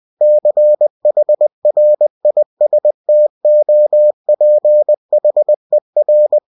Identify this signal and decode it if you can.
Morse code, heard as a keyed tone, about 20 wpm, 600 Hz. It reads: CHRISTOPHER
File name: cw-books / winnie-the-pooh / winnie-20wpm-page-0004